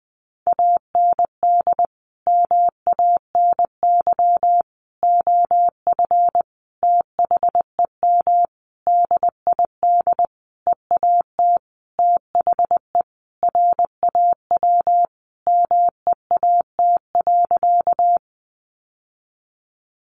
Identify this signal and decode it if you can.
Morse code, heard as a keyed tone, about 20 wpm, 700 Hz. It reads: AND MANY OF THEM DID EAT THE RAW MEAT.
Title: Sentences from Top 1500 Words 20wpm